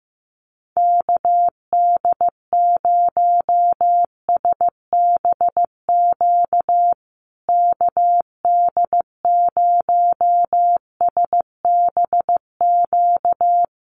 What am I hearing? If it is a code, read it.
Morse code, heard as a keyed tone, about 15 wpm, 700 Hz. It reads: KD0SBQ KD0SBQ
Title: USA Call Signs Repeated 3x 15wpm